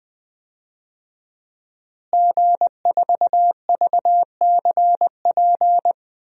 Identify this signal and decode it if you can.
Morse code, heard as a keyed tone, about 21 wpm, 700 Hz. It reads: G4VCP
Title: DXCC-100 Call Signs - 5 Characters - Repeated 2x - 20wpm